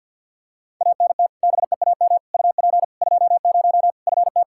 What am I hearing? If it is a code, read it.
Morse code, heard as a keyed tone, about 47 wpm, 700 Hz. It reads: ANT BEAM UP 20 FT
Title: Practice QSO - Exchange 2 - Info - 50wpm